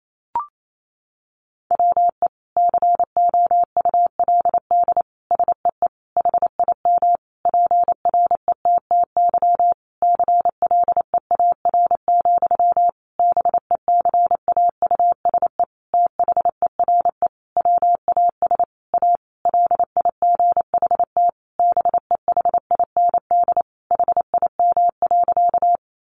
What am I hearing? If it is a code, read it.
Morse code, heard as a keyed tone, about 28 wpm, 700 Hz. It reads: WE COULD SEE HIM PRETTY CLEAR, BECAUSE THERE WAS A LIGHT BEHIND HIM.
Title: Sentences from Top 1000 Words 28wpm